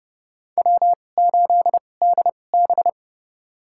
Morse code, heard as a keyed tone, about 32 wpm, spaced as for 22 wpm, 700 Hz. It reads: W8DB
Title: USA Call Signs - 4 Characters - 2x Character Spacing - 30wpm